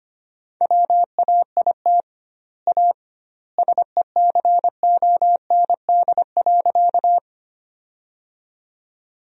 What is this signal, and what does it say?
Morse code, heard as a keyed tone, about 25 wpm, 700 Hz. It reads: WAIT A SECOND.
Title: 2x Word-Spacing - Sentences from Top 500 Words - 25wpm